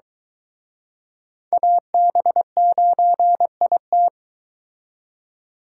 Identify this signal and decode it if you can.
Morse code, heard as a keyed tone, about 23 wpm, 700 Hz. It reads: AB9IT